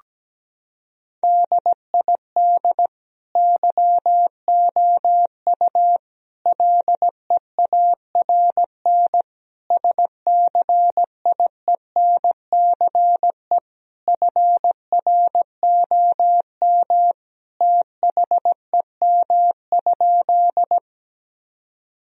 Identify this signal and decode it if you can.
Morse code, heard as a keyed tone, about 17 wpm, 700 Hz. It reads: DID YOU LEARN SCIENCE FROM THEM?
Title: Sentences from Top 1000 Words - Review Part 2 - 17wpm